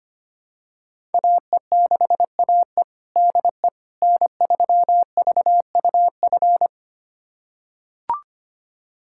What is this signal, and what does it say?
Morse code, heard as a keyed tone, about 25 wpm, 700 Hz. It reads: AE6AE DE N3VUF
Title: USA Callsign DE Callsign - 5 Characters - 25wpm